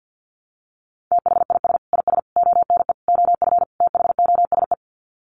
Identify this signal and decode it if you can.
Morse code, heard as a keyed tone, about 50 wpm, 700 Hz. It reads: THIS IS ONE OF THOSE
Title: Sets of 5 Words Encore2 50wpm